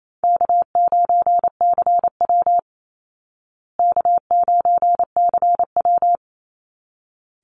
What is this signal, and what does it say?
Morse code, heard as a keyed tone, about 28 wpm, 700 Hz. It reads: K9CW K9CW
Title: Easy Call Signs Repeated 2x - 28wpm